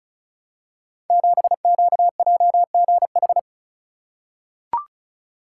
Morse code, heard as a keyed tone, about 35 wpm, 690 Hz. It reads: 7QJGH
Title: Code Groups - 5 Letter:Numbers - 35wpm